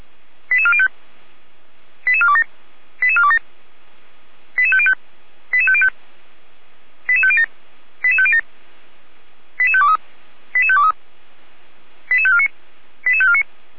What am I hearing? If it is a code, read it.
Five-tone selective call (ZVEI), heard as quick runs of five tones, tones of about 70 ms. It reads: 88405 88427 88427 88505 88505 88507 88507 88523 88523 88549 88549
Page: audio.f85360.feuerwehr-orlamuende.org